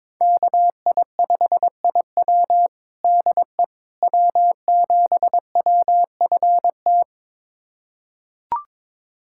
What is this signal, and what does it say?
Morse code, heard as a keyed tone, about 22 wpm, 700 Hz. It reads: KI5IW DE W7WFT
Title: USA Callsign DE Callsign 22wpm